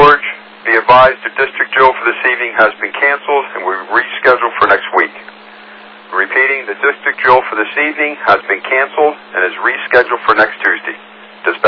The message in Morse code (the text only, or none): none